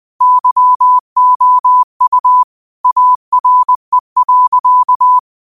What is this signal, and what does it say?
Morse code, heard as a keyed tone, about 20 wpm, 1000 Hz. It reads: YOU ARE.